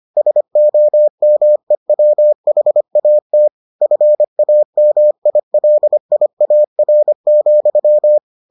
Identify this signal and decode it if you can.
Morse code, heard as a keyed tone, about 25 wpm, 600 Hz. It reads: SOMEWHAT FAMILIAR,